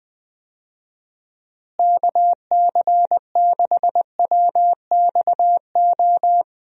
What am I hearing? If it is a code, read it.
Morse code, heard as a keyed tone, about 21 wpm, 700 Hz. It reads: KC6WXO
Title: USA Call Signs - 6 Characters - Repeated 2x - 20wpm